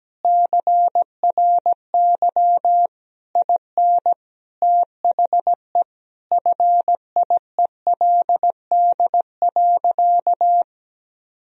Morse code, heard as a keyed tone, about 17 wpm, 700 Hz. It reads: CRY IN THE FIELD.
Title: Sentences from Top 1000 Words - Review Part 2 - 17wpm